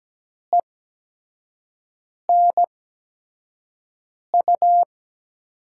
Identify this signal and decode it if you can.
Morse code, heard as a keyed tone, about 17 wpm, spaced as for 5 wpm, 700 Hz. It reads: ENU
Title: USA Call Signs - 6 Characters - 8x Character Spacing - 17wpm